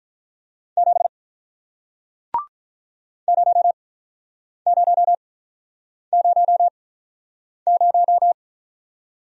Morse code, heard as a keyed tone, about 48 wpm, spaced as for 9 wpm, 700 Hz. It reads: 7 0000